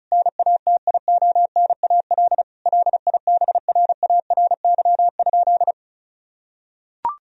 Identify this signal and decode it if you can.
Morse code, heard as a keyed tone, about 36 wpm, 700 Hz. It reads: NATIONAL LIBRARY?
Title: Sentences from Top 1000 Words - Review Part 4 - 35wpm